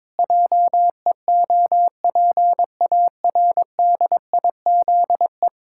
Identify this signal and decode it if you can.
Morse code, heard as a keyed tone, about 22 wpm, 700 Hz. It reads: JEOPARDIZE
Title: Pangram Challenge v2 - 22wpm